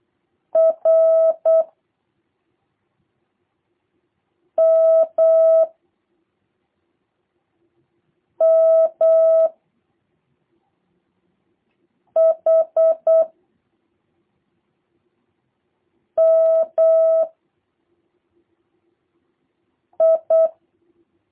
Morse code, heard as a keyed tone, about 8 wpm, 650 Hz. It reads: R M M H M I
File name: jh8cbh / 2023cw / review1-2